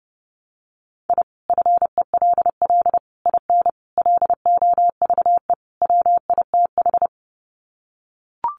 Morse code, heard as a keyed tone, about 30 wpm, 700 Hz. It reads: I FELL IN LOVE WITH
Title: Sets of 5 Words Encore 30wpm